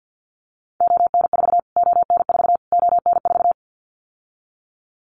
Morse code, heard as a keyed tone, about 49 wpm, 700 Hz. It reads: ON4 ON4 ON4
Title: Callsign Prefixes for Top 25 DXCC - Repeated 3x - 50wpm